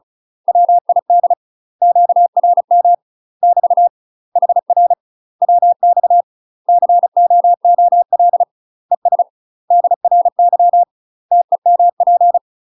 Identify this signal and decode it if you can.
Morse code, heard as a keyed tone, about 34 wpm, 700 Hz. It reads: WID QRM <BT> HR WX COOL ES DRY TEMP
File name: UK MASTER QSOs TRAINING / 35WPM / UK MASTER QSOs 0037